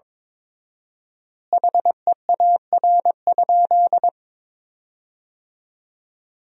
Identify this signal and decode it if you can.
Morse code, heard as a keyed tone, about 22 wpm, 700 Hz. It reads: HEAR?
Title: 4x Word-Spacing - Sentences from Top 300 Words - 22wpm